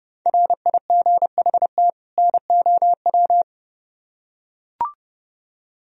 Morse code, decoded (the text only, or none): RIGHT NOW